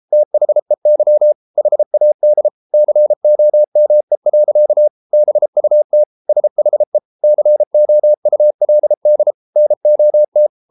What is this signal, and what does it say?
Morse code, heard as a keyed tone, about 33 wpm, 600 Hz. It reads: THEY HAD COME. BUT SHE COULD NOT